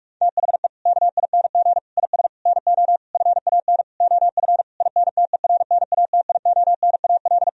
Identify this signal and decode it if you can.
Morse code, heard as a keyed tone, about 46 wpm, 700 Hz. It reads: THE KING IS NO FAN OF INTERNATIONAL
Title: Sentences from Top 1000 Words - Review Part 4 - 45wpm